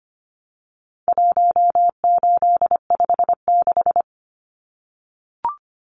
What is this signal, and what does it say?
Morse code, heard as a keyed tone, about 25 wpm, 700 Hz. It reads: 1856
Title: Numbers - 4-Digit 25wpm